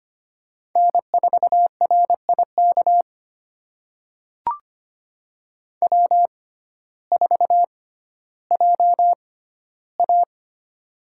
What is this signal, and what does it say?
Morse code, heard as a keyed tone, about 25 wpm, spaced as for 14 wpm, 700 Hz. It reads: N4RIK W4JA
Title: USA Call Signs - 6x Character Spacing - 25wpm